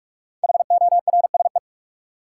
Morse code, heard as a keyed tone, about 41 wpm, 700 Hz. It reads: HORSE
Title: Five Letter Words - 45wpm